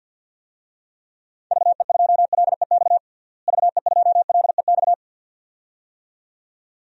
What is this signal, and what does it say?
Morse code, heard as a keyed tone, about 53 wpm, 700 Hz. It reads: VE2LEX VE2LEX
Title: DXCC-100 Call Signs - 6 Characters - Repeated 2x - 50wpm